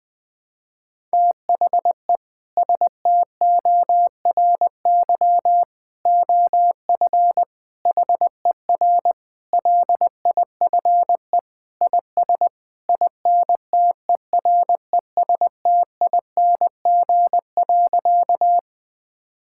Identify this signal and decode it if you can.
Morse code, heard as a keyed tone, about 21 wpm, 700 Hz. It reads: THE STORY OF HER LIFE IS INTERESTING.
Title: Sentences with 3-Word Phrases - 20wpm